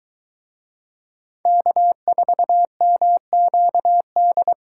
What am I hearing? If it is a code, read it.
Morse code, heard as a keyed tone, about 23 wpm, 700 Hz. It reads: K4MQD